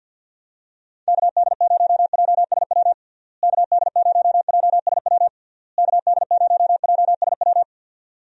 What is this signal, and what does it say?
Morse code, heard as a keyed tone, about 53 wpm, 700 Hz. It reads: KD0JSW KD0JSW KD0JSW
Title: USA Call Signs - 6 Characters - Repeated 3x - 50wpm